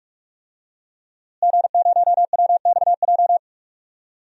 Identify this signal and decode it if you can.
Morse code, heard as a keyed tone, about 43 wpm, 700 Hz. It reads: G0WXJ
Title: DXCC-100 Call Signs - 5 Characters - Repeated 2x - 45wpm